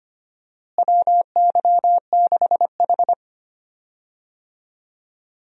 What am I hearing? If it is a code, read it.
Morse code, heard as a keyed tone, about 25 wpm, 700 Hz. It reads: WY6H
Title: USA Call Signs - 4 Characters - Repeated 3x - 25wpm